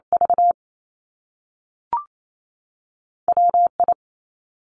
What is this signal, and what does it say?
Morse code, heard as a keyed tone, about 28 wpm, 700 Hz. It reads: V WI
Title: Top Two Letter Combinations in Words 28wpm